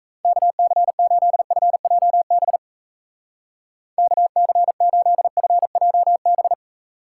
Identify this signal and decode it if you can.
Morse code, heard as a keyed tone, about 33 wpm, 700 Hz. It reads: KC8FJB KC8FJB
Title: Speed Racing - USA 2x3 Call Signs 28wpm